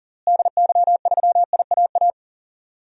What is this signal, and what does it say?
Morse code, heard as a keyed tone, about 40 wpm, 700 Hz. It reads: DY3IAA
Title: DXCC-100 Call Signs - 6 Characters - Repeated 3x - 40wpm